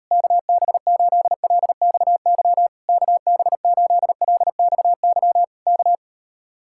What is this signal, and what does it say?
Morse code, heard as a keyed tone, about 38 wpm, 700 Hz. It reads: KB8LXY KB8LXY K